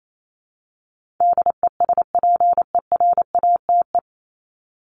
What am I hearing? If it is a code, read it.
Morse code, heard as a keyed tone, about 28 wpm, 700 Hz. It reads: DESPERATE
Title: Top 1751-2000 Words 28wpm